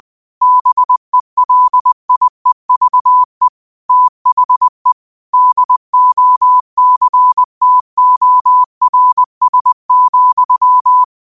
Morse code, heard as a keyed tone, about 20 wpm, 1000 Hz. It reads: BELIEVE THE DOCTORS,